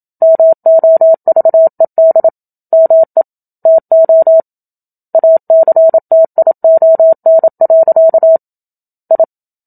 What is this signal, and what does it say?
Morse code, heard as a keyed tone, about 27 wpm, 650 Hz. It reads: MOVED ME TO ACTION. I
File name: Book Two / 01-Chapter 1 - 27 WPM